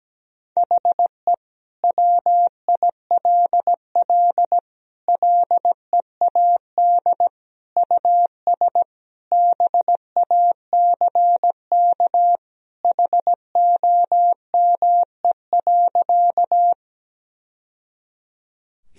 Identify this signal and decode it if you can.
Morse code, heard as a keyed tone, about 17 wpm, 700 Hz. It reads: HE WILL LEAD US BACK HOME.
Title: Sentences from Top 1000 Words - Review Part 2 - 17wpm